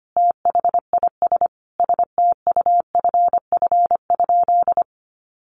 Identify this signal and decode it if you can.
Morse code, heard as a keyed tone, about 25 wpm, 700 Hz. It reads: THIS STUFF?
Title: Sentences from Top 1400 Words 25wpm